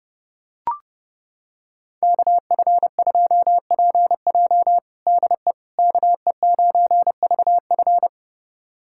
Morse code, heard as a keyed tone, about 30 wpm, 700 Hz. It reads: KF2PJ DE KE9VF
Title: USA Callsign DE Callsign - 5 Characters - 30wpm